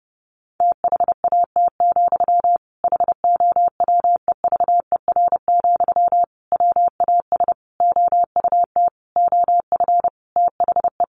Morse code, heard as a keyed tone, about 30 wpm, 700 Hz. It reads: THAT, HOWEVER, WAS OUT OF THE
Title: Sentences from Top 600 Words 30wpm